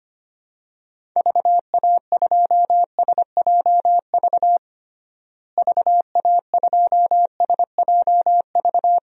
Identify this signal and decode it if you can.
Morse code, heard as a keyed tone, about 25 wpm, 700 Hz. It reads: VA2SJV VA2SJV
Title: DXCC-100 Call Signs - 6 Characters - Repeated 2x - 25wpm